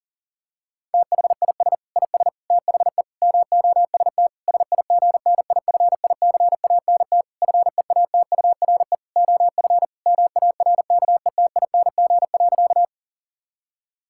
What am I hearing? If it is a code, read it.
Morse code, heard as a keyed tone, about 40 wpm, 700 Hz. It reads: THIS IS THE MOST SIGNIFICANT FEATURE OF MARKETING.
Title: Sentences from Top 1000 Words - Review Part 6 - 40wpm